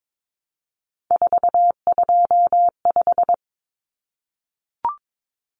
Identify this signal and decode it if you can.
Morse code, heard as a keyed tone, about 22 wpm, 700 Hz. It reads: 425
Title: Numbers - 3-Digit 22wpm